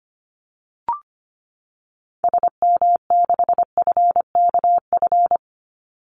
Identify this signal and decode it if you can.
Morse code, heard as a keyed tone, about 25 wpm, 700 Hz. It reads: SM6FKF
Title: Call Signs - 25wpm